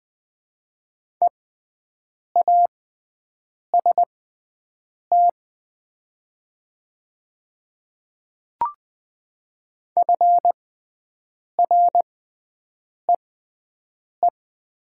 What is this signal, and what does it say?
Morse code, heard as a keyed tone, about 20 wpm, spaced as for 7 wpm, 700 Hz. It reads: EAST FREE
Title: 6x Character Spacing - Top 500 Words - 20wpm